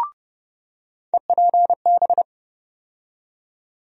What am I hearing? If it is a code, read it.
Morse code, heard as a keyed tone, about 30 wpm, 700 Hz. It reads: EPB
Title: USA Call Sign Postfixes - 3 Letters 30wpm